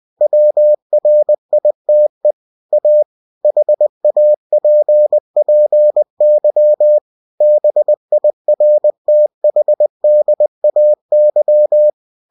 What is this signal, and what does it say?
Morse code, heard as a keyed tone, about 20 wpm, 600 Hz. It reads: WRITE A HAPPY BIRTHDAY
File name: cw-books / winnie-the-pooh / winnie-20wpm-page-0051